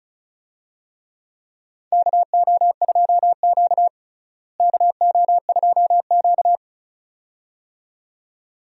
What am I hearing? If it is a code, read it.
Morse code, heard as a keyed tone, about 36 wpm, 700 Hz. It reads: KO2Q KO2Q
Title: USA Call Signs - 4 Characters - Repeated 2x - 35wpm